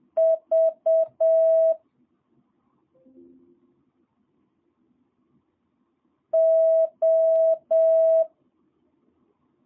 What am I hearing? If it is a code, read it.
Morse code, heard as a keyed tone, about 7 wpm, 650 Hz. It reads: V O
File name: jh8cbh / 2022cw